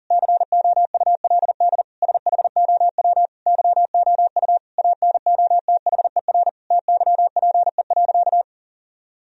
Morse code, heard as a keyed tone, about 37 wpm, 700 Hz. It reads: COULD SHOW YOU ANOTHER TYPE.
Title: Sentences from Top 1000 Words - Review Part 3 - 40wpm